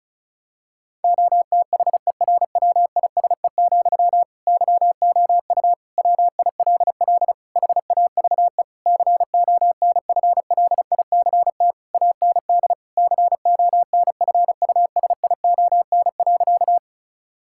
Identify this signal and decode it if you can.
Morse code, heard as a keyed tone, about 36 wpm, 700 Hz. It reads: OTHERWISE, YOU WILL HAVE CONFLICT AND CONFUSION.
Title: Sentences from Top 1600 Words 35wpm